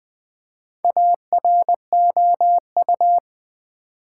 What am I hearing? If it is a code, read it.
Morse code, heard as a keyed tone, about 20 wpm, 690 Hz. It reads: AROU
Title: Build A Word - 2 Letters At A Time - 6-Letter Words - 20wpm